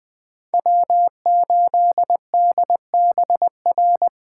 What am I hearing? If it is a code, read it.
Morse code, heard as a keyed tone, about 24 wpm, 700 Hz. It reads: W8DBR
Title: Speed Racing - USA 1x3 Call Signs 20wpm